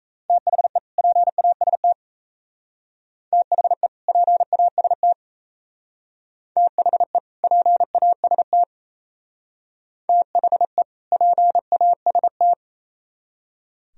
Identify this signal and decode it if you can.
Morse code, heard as a keyed tone, about 36 wpm, 700 Hz. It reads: THE PAST THE PAST THE PAST THE PAST